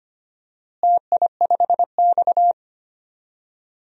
Morse code, heard as a keyed tone, about 25 wpm, 700 Hz. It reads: TI5X
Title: DXCC-100 Call Signs - 4 Characters - Repeated 2x - 25wpm